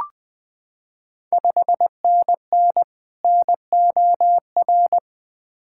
Morse code, heard as a keyed tone, about 20 wpm, 700 Hz. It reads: 5NN NOR
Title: Ontario QSO Party Exchange 20wpm